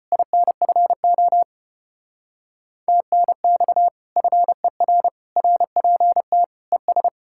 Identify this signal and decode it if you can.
Morse code, heard as a keyed tone, about 33 wpm, 700 Hz. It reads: INFO TNX FER RPT ES